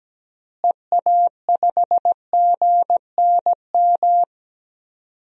Morse code, heard as a keyed tone, about 17 wpm, 700 Hz. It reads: EA5GNM
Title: DXCC-100 Call Signs - 6 Characters - Repeated 2x - 17wpm